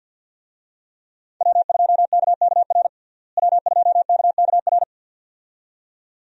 Morse code, heard as a keyed tone, about 47 wpm, 700 Hz. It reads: W2KKR W2KKR
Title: USA Call Signs - 5 Characters - Repeated 2x - 50wpm